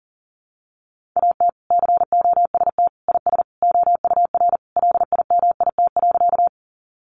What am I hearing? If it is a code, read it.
Morse code, heard as a keyed tone, about 40 wpm, 700 Hz. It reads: AT COST IS OUR LIMIT.
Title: Sentences from Top 1500 Words 40wpm